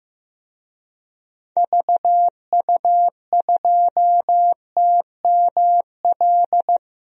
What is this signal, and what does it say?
Morse code, heard as a keyed tone, about 15 wpm, 700 Hz. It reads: VU2TML
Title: DXCC-100 Call Signs - 6 Characters - Repeated 2x - 15wpm